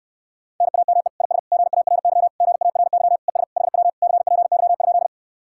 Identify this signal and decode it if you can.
Morse code, heard as a keyed tone, about 68 wpm, 700 Hz. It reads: NAME IS GARY GARY ES HW CPY?